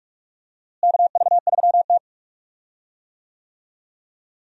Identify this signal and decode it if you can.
Morse code, heard as a keyed tone, about 43 wpm, 700 Hz. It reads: KV3T
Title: USA Call Signs - 4 Characters - Repeated 2x - 45wpm